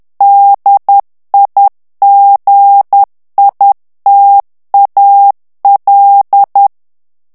Morse code, heard as a keyed tone, about 10 wpm, 800 Hz. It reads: DIGITAL
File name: ANGLAIS texte clair 3 vitesse 10 mots minute